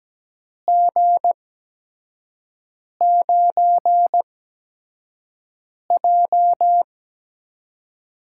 Morse code, heard as a keyed tone, about 17 wpm, spaced as for 5 wpm, 700 Hz. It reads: G9J
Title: USA Call Signs - 8x Character Spacing - 17wpm